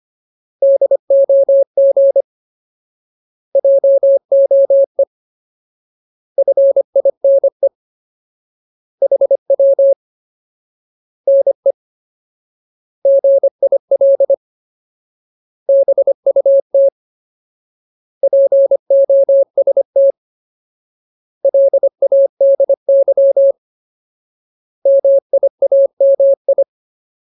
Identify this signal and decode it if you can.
Morse code, heard as a keyed tone, about 25 wpm, 550 Hz. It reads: DOG JOE FINE HW NE GIL BUT POST LADY MIAMI
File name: Ultimate Word List / 25WPM Ultimate